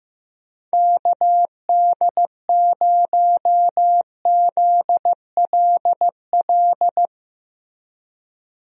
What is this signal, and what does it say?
Morse code, heard as a keyed tone, about 15 wpm, 700 Hz. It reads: KD0ZLL
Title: USA Call Signs - 6 Characters - Repeated 2x - 15wpm